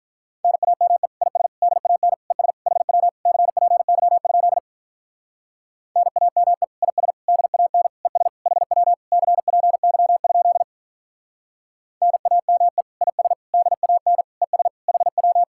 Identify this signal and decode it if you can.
Morse code, heard as a keyed tone, about 43 wpm, 700 Hz. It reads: NAME IS DAN ES HW CPY? NAME IS DAN ES HW CPY? NAME IS DAN ES HW
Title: Speed-Racing Common QSO Phrases - 35wpm